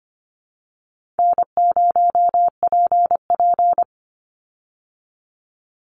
Morse code, heard as a keyed tone, about 25 wpm, 700 Hz. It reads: N0PP
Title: Call Signs - 25wpm